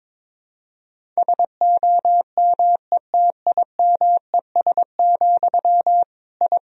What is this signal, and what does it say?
Morse code, heard as a keyed tone, about 22 wpm, 700 Hz. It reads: SOMETIMES, I
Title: Sentences from Top 1000 Words - Review Part 3 - 22wpm